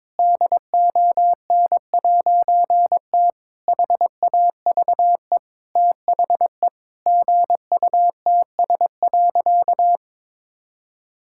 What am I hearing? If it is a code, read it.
Morse code, heard as a keyed tone, about 21 wpm, 700 Hz. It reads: DON'T HAVE THE GUTS.
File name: All-Age Appropriate Jokes - 22wpm